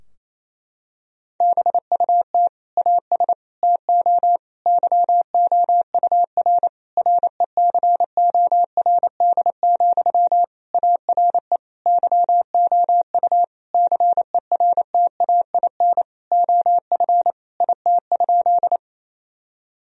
Morse code, heard as a keyed tone, about 28 wpm, 700 Hz. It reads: BUT AS TO YOUR RECORD, ARE YOU CERTAIN OF IT?
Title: Sentences from Top 1000 Words - Review Part 3 - 28wpm